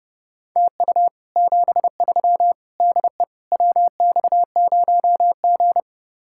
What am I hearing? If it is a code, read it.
Morse code, heard as a keyed tone, about 32 wpm, 700 Hz. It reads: TU 73 DE WX0G